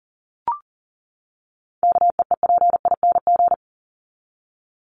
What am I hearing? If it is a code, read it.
Morse code, heard as a keyed tone, about 40 wpm, 700 Hz. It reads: KEEPING
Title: ING Suffix - Top 100 Words with Base Words 40wpm